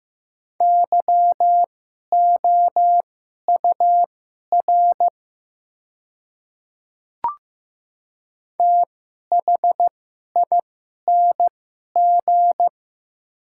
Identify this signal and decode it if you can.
Morse code, heard as a keyed tone, about 15 wpm, spaced as for 11 wpm, 690 Hz. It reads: YOUR THING
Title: 2x Character Spacing - Top 100 Words - 15wpm